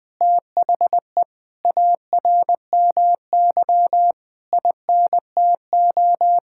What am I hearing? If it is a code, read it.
Morse code, heard as a keyed tone, about 21 wpm, 700 Hz. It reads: THE ARMY INTO
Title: Sentences from Top 1000 Words - Review Part 5 - 20wpm